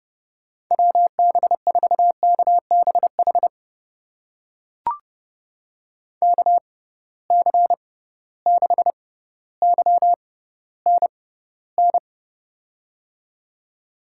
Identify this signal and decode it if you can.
Morse code, heard as a keyed tone, about 30 wpm, spaced as for 15 wpm, 700 Hz. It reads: WB4KBH KC6YNN
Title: USA Call Signs - 6 Characters - 6x Character Spacing - 30wpm